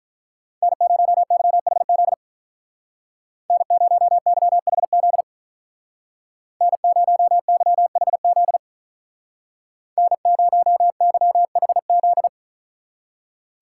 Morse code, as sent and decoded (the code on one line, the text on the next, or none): -. ----- -.-- .... --.. / -. ----- -.-- .... --.. / -. ----- -.-- .... --.. / -. ----- -.-- .... --..
N0YHZ N0YHZ N0YHZ N0YHZ